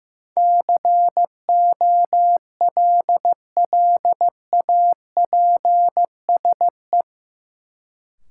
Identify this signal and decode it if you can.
Morse code, heard as a keyed tone, about 15 wpm, 700 Hz. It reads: COLLAPSE
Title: Sets of 5 Words Encore - New Word Drill - 15wpm